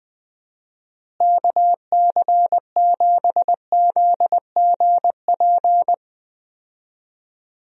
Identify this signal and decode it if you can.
Morse code, heard as a keyed tone, about 20 wpm, 700 Hz. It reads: KC7ZGP
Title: USA Call Signs - 6 Characters - Repeated 2x - 20wpm